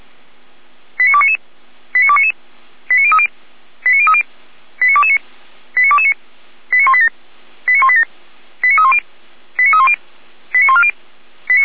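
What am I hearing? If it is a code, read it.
Five-tone selective call (ZVEI), heard as quick runs of five tones, tones of about 70 ms. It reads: 89299 89299 89030 89030 89229 89229 89187 89187 89310 89310 89140
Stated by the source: Five-tone ZVEI selective call received by SDR, from the audio archive of Feuerwehr Orlamünde (fire brigade).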